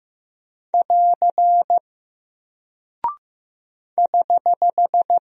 Morse code, heard as a keyed tone, about 15 wpm, 700 Hz. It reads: <AR> <HH>
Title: Single Character (Letter:Number:Punctuation:Prosign) Alt 15wpm